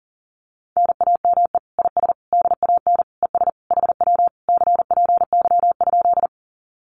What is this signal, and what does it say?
Morse code, heard as a keyed tone, about 40 wpm, 700 Hz. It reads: NAME IS DAN ES HW CPY?
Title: Common QSO Phrases Encore 40wpm